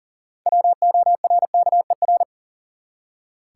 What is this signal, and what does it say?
Morse code, heard as a keyed tone, about 40 wpm, 700 Hz. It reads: WORKER
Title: Six Letter Words - 40wpm